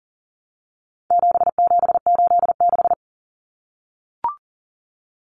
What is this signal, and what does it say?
Morse code, heard as a keyed tone, about 40 wpm, 700 Hz. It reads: 7786
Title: Numbers - 4-Digit 40wpm